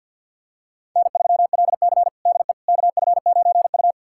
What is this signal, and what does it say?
Morse code, heard as a keyed tone, about 53 wpm, 700 Hz. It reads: N3LC DE KF9U
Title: USA Callsign DE Callsign - 4 Characters - 50wpm